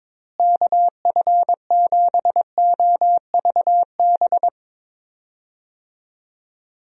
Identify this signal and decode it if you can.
Morse code, heard as a keyed tone, about 21 wpm, 700 Hz. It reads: KF7OVB
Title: USA Call Signs - 6 Characters - Repeated 2x - 22wpm